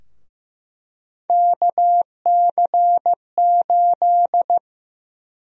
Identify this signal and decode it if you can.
Morse code, heard as a keyed tone, about 15 wpm, 700 Hz. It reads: KC8
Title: USA Call Sign Prefixes - Repeated 2x - 15wpm